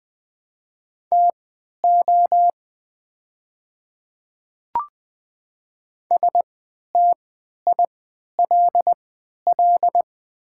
Morse code, heard as a keyed tone, about 20 wpm, spaced as for 11 wpm, 690 Hz. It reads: TO STILL
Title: 3x Character Spacing - Top 500 Words - 20wpm